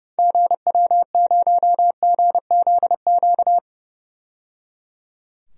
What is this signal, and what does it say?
Morse code, heard as a keyed tone, about 29 wpm, 700 Hz. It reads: GW0GZQ
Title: DXCC-100 Call Signs - 6 Characters - 30wpm